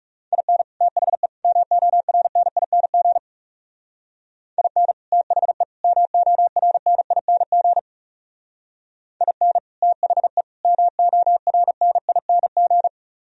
Morse code, heard as a keyed tone, about 39 wpm, 700 Hz. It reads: IN THE MORNING IN THE MORNING IN THE MORNING